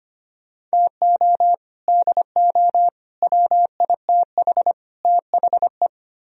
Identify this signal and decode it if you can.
Morse code, heard as a keyed tone, about 32 wpm, 700 Hz. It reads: TO DO WITH THE